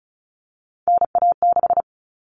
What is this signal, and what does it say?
Morse code, heard as a keyed tone, about 34 wpm, 700 Hz. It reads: NA6
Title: Common Cut Numbers - 3-Digit 35wpm